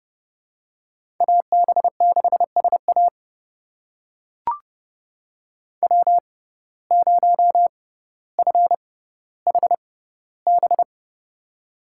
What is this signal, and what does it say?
Morse code, heard as a keyed tone, about 30 wpm, spaced as for 15 wpm, 700 Hz. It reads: AB6SA W0FHB
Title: USA Call Signs - 5 Characters - 6x Character Spacing - 30wpm